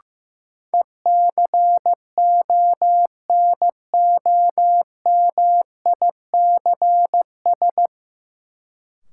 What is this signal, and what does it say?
Morse code, heard as a keyed tone, about 15 wpm, 700 Hz. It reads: ECONOMICS
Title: Sentences from Top 1000 Words - Review Part 3 - New Word Drill - 15wpm